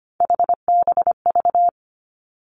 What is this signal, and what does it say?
Morse code, heard as a keyed tone, about 25 wpm, 700 Hz. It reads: HBV
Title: Three Challenging Letter-Number Combinations 25wpm